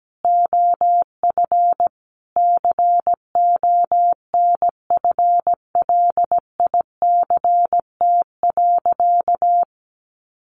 Morse code, heard as a keyed tone, about 17 wpm, 700 Hz. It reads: OF CONFLICT.
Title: Sentences from Top 1400 Words 17wpm